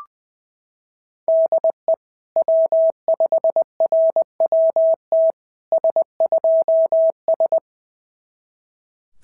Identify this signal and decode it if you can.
Morse code, heard as a keyed tone, about 18 wpm, 650 Hz. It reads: DE W5RWT S2S
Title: SOTA QSO Practice 20wpm